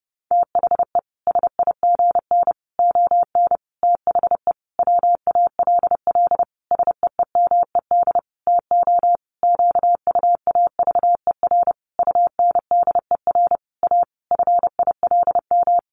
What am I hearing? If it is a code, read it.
Morse code, heard as a keyed tone, about 30 wpm, 700 Hz. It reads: THE SIGN ON THE WALL SEEMED TO QUAVER UNDER A FILM
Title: A Sound Of Thunder 30wpm